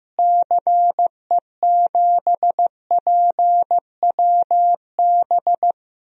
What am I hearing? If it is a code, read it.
Morse code, heard as a keyed tone, about 15 wpm, 700 Hz. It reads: CE7PWB
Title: DXCC-100 Call Signs - 6 Characters - Repeated 2x - 15wpm